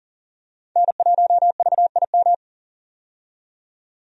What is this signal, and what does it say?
Morse code, heard as a keyed tone, about 40 wpm, 700 Hz. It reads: N1VIM
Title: USA Call Signs - 5 Characters - 40wpm